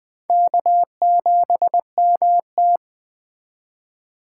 Morse code, heard as a keyed tone, about 21 wpm, 700 Hz. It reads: K7MT
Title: Speed Racing - USA 1x2 Call Signs 17wpm